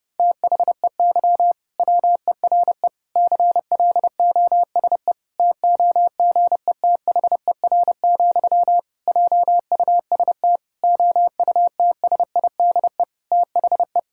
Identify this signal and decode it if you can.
Morse code, heard as a keyed tone, about 30 wpm, 700 Hz. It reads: THEY WERE CLOSE TOGETHER, JUST OUTSIDE THE
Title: Sentences from Top 1000 Words - Review Part 6 - 30wpm